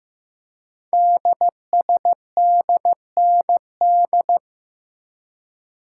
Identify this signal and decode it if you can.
Morse code, heard as a keyed tone, about 15 wpm, 700 Hz. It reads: DSDND
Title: Code Groups - 5 Letters - 15wpm